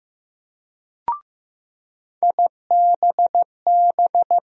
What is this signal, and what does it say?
Morse code, heard as a keyed tone, about 15 wpm, 700 Hz. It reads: IBB
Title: USA Call Sign Postfixes - 3 Letters 15wpm